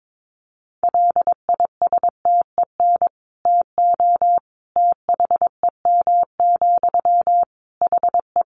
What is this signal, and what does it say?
Morse code, heard as a keyed tone, about 22 wpm, 700 Hz. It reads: LISTEN TO THEM, HE
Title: Sentences from Top 400 Words 22wpm